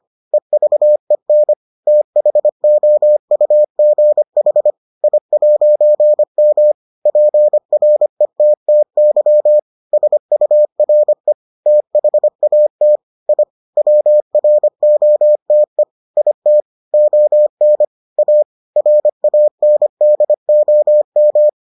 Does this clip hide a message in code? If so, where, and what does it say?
Yes: Morse code from 0.3 to 21.5 s, heard as a keyed tone, about 25 wpm, 600 Hz. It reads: EVEN THOUGH I'M PRETTY SURE THAT I WROTE IT ON A RANDOM